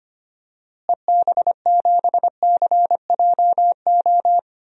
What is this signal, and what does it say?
Morse code, heard as a keyed tone, about 25 wpm, 690 Hz. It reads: EB7CJO
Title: DXCC-100 Call Signs - 6 Characters - Repeated 3x - 25wpm